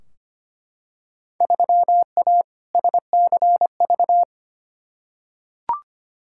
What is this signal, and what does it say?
Morse code, heard as a keyed tone, about 25 wpm, 700 Hz. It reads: 3A SCV
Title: Field Day Exchanges - 25wpm